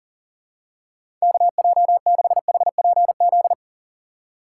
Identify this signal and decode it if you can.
Morse code, heard as a keyed tone, about 40 wpm, 700 Hz. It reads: KJ6HPZ
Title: USA Call Signs - 6 Characters - 6x Character Spacing - 40wpm